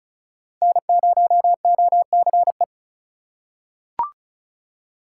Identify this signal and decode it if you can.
Morse code, heard as a keyed tone, about 36 wpm, 700 Hz. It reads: N0OCE
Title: USA Call Signs - 5 Characters - 4x Character Spacing - 35wpm